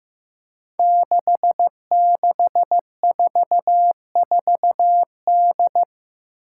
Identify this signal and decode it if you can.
Morse code, heard as a keyed tone, about 15 wpm, 700 Hz. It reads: 6644D
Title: Five Challenging Letter-Number Combinations 15wpm